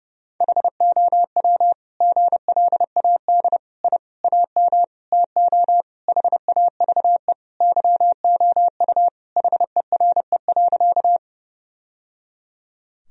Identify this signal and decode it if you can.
Morse code, heard as a keyed tone, about 30 wpm, 700 Hz. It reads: HOW GLAD I AM TO HAVE YOU HERE.
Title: Sentences from Top 1000 Words - Review Part 2 - 30wpm